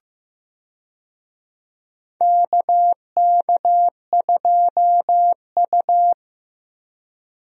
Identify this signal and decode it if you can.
Morse code, heard as a keyed tone, about 15 wpm, 700 Hz. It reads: KK2U
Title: USA Call Signs - 4 Characters - Repeated 2x - 15wpm